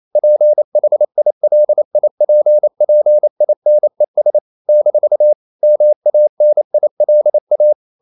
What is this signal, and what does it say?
Morse code, heard as a keyed tone, about 28 wpm, 600 Hz. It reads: PHILIPPINES - MANILA
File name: CW / Practice / NationalCapitals28